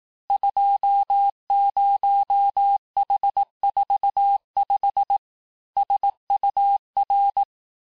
Morse code, heard as a keyed tone, about 18 wpm, 800 Hz. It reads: 20H45 SUR
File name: cours-du-lundi files / cours-1588-texte-chiffre-vitesse-17MPM